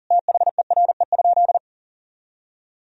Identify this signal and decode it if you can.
Morse code, heard as a keyed tone, about 40 wpm, 700 Hz. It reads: THERE?
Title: Sentences from Top 1000 Words - Review Part 4 - 40wpm